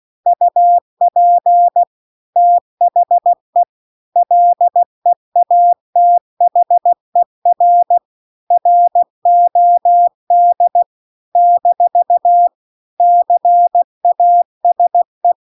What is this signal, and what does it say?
Morse code, heard as a keyed tone, about 16 wpm, 700 Hz. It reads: UP THE LEATHER ROD - CASE